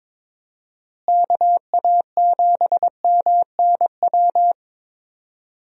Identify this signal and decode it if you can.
Morse code, heard as a keyed tone, about 22 wpm, 700 Hz. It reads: KA7MNW
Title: USA Call Signs - 6 Characters - Repeated 3x - 22wpm